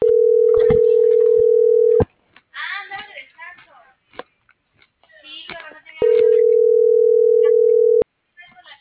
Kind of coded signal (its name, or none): telephone call progress tone